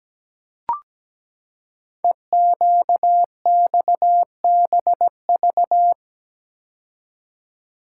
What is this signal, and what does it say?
Morse code, heard as a keyed tone, about 17 wpm, 700 Hz. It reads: EQXBV
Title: Code Groups - 5 Letter:Numbers - 17wpm